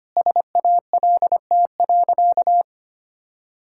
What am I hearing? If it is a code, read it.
Morse code, heard as a keyed tone, about 25 wpm, 700 Hz. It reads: SALT.
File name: Easy Sentences with Hints - 25wpm